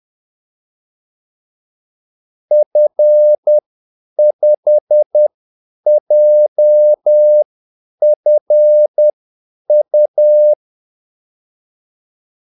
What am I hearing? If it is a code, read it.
Morse code, heard as a keyed tone, about 10 wpm, spaced as for 8 wpm, 600 Hz. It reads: F5JFU
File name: Callsigns-10-8wpm